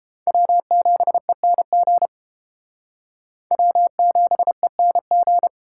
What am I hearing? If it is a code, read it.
Morse code, heard as a keyed tone, about 29 wpm, 700 Hz. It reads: W7ENG W7ENG